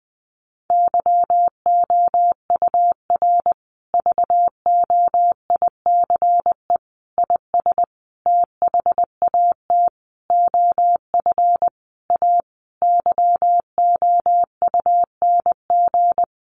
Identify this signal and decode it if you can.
Morse code, heard as a keyed tone, about 19 wpm, 700 Hz. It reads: YOUR VOICE IS THAT OF A YOUNG
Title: Sentences from Top 500 Words 20wpm